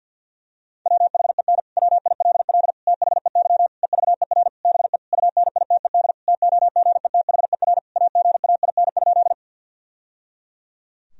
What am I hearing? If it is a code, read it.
Morse code, heard as a keyed tone, about 51 wpm, 700 Hz. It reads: WHEN WILL THEY EVER BE UNITED TOGETHER AGAIN?
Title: Sentences from Top 1000 Words - Review Part 4 - 50wpm